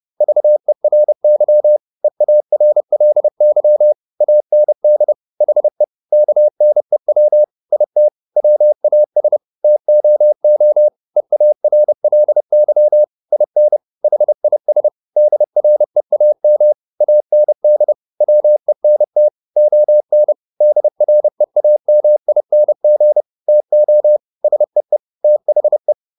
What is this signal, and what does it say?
Morse code, heard as a keyed tone, about 30 wpm, 600 Hz. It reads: VERY EARLY AND HE KNEW IT WAS TOO EARLY IN HIS DREAM AND WENT ON DREAMING TO SEE THE
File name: the old man and the sea / the old man and the sea 1 30wpm